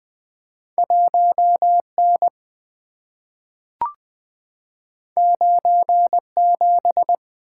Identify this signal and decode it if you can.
Morse code, heard as a keyed tone, about 20 wpm, 700 Hz. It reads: 1N 97
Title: Two Letter-Number Combinations - 20wpm